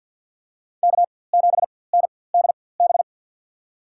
Morse code, heard as a keyed tone, about 47 wpm, spaced as for 23 wpm, 690 Hz. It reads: K7NDB
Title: USA Call Signs - 5 Characters - 4x Character Spacing - 50wpm